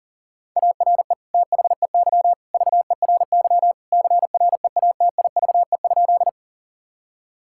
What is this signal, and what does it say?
Morse code, heard as a keyed tone, about 40 wpm, 700 Hz. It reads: ARE THEY VERY CREATIVE?
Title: Sentences from Top 1000 Words - Review Part 5 - 40wpm